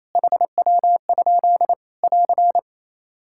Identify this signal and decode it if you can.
Morse code, heard as a keyed tone, about 28 wpm, 700 Hz. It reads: HW? <AR>
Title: Practice QSO - Exchange 3 - More Info - 28wpm